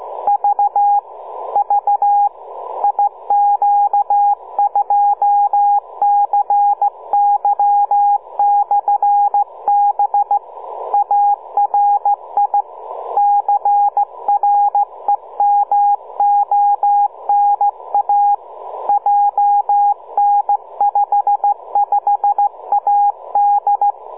Morse code, heard as a keyed tone, about 14 wpm, 820 Hz, with radio noise. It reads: V V IQ2CY/B ARI CREMONA JN55AD